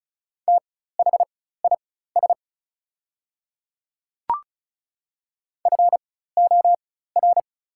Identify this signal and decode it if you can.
Morse code, heard as a keyed tone, about 36 wpm, spaced as for 16 wpm, 700 Hz. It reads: THIS FOR
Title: 4x Character Spacing - Top 200 Words - 35wpm